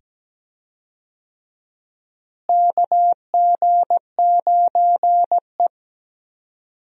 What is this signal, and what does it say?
Morse code, heard as a keyed tone, about 17 wpm, 700 Hz. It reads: KG9E
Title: USA Call Signs - 4 Characters - Repeated 2x - 17wpm